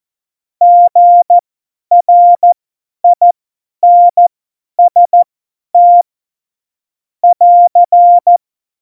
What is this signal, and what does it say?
Morse code, heard as a keyed tone, about 14 wpm, spaced as for 10 wpm, 700 Hz. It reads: GRINST <AR>
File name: Morsekurs / Witze / Witz 02. Oktober 2018, 50 F70